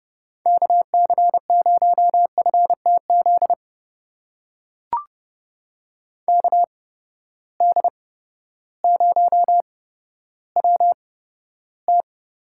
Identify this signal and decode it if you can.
Morse code, heard as a keyed tone, about 31 wpm, spaced as for 14 wpm, 700 Hz. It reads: KC0FTZ KD0WT
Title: USA Call Signs - 6 Characters - 8x Character Spacing - 30wpm